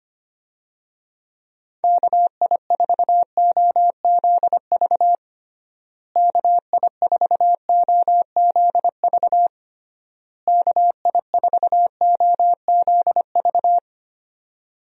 Morse code, heard as a keyed tone, about 25 wpm, 700 Hz. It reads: KI4OZV KI4OZV KI4OZV
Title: USA Call Signs - 6 Characters - Repeated 3x - 25wpm